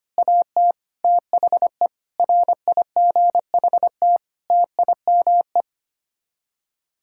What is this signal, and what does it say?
Morse code, heard as a keyed tone, about 25 wpm, 700 Hz. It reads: AT THE RIGHT TIME
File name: Sets of 9 Words - 25wpm